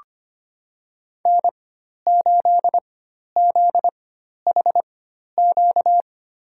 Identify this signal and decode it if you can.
Morse code, heard as a keyed tone, about 25 wpm, spaced as for 12 wpm, 700 Hz. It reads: N8ZHQ
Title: USA Call Signs - 5 Characters - 4x Character Spacing - 25wpm